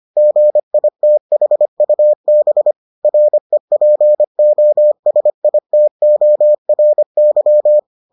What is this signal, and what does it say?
Morse code, heard as a keyed tone, about 25 wpm, 600 Hz. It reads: GITHUB REPOSITORY